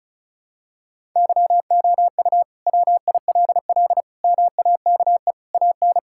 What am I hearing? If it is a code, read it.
Morse code, heard as a keyed tone, about 36 wpm, 700 Hz. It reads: YOU WILL MAKE AN
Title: Sentences from Top 1000 Words - Review Part 5 - 35wpm